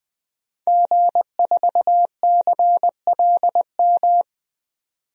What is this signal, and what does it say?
Morse code, heard as a keyed tone, about 20 wpm, 700 Hz. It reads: G4CLM
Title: DXCC-100 Call Signs - 5 Characters - 20wpm